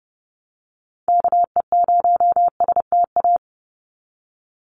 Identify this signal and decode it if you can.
Morse code, heard as a keyed tone, about 30 wpm, 700 Hz. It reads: KE0STA
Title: USA Call Signs 30wpm